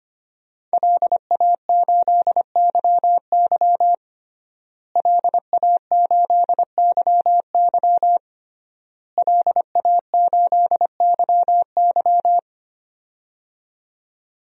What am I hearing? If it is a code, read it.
Morse code, heard as a keyed tone, about 25 wpm, 700 Hz. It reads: LA8YY LA8YY LA8YY
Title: DXCC-100 Call Signs - 5 Characters - Repeated 3x - 25wpm